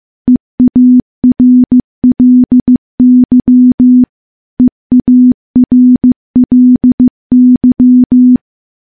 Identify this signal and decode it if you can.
Morse code, heard as a keyed tone, about 15 wpm, 250 Hz. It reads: EARLY EARLY
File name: morse / morse-training-15WPM